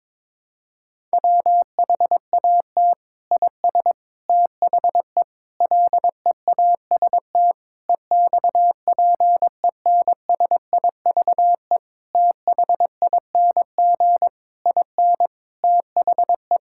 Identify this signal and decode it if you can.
Morse code, heard as a keyed tone, about 22 wpm, 700 Hz. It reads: WHAT IS THE LEAST EXPENSIVE THING IN THE
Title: Sentences from Top 1000 Words - Review Part 6 - 22wpm